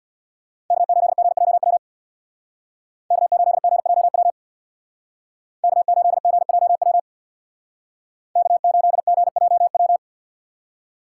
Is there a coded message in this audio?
Yes: Morse code from 0.7 to 10.0 s, heard as a keyed tone, about 64 wpm, 700 Hz. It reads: K8GJW K8GJW K8GJW K8GJW